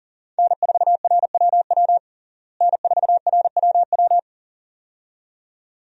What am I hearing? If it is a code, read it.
Morse code, heard as a keyed tone, about 40 wpm, 700 Hz. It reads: N4RWW N4RWW
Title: USA Call Signs - 5 Characters - Repeated 2x - 40wpm